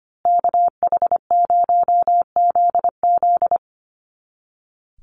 Morse code, heard as a keyed tone, about 25 wpm, 700 Hz. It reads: KH0ZZ
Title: Call Signs Repeated 2x - 25wpm